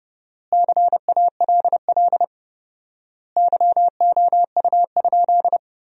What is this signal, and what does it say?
Morse code, heard as a keyed tone, about 30 wpm, 700 Hz. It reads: CALL YOU?
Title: 4x Word-Spacing - Sentences from Top 100 Words - 30wpm